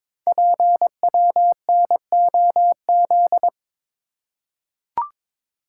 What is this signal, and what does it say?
Morse code, heard as a keyed tone, about 22 wpm, 700 Hz. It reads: PWNOZ
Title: Code Groups - 5 Letters - 22wpm